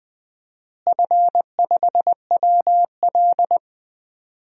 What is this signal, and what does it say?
Morse code, heard as a keyed tone, about 20 wpm, 700 Hz. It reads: F5WL
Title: DXCC-100 Call Signs - 4 Characters - Repeated 3x - 20wpm